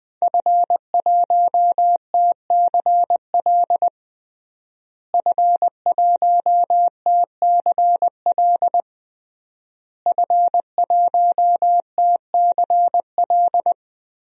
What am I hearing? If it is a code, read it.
Morse code, heard as a keyed tone, about 21 wpm, 700 Hz. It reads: F1TCL F1TCL F1TCL
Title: DXCC-100 Call Signs - 5 Characters - Repeated 3x - 20wpm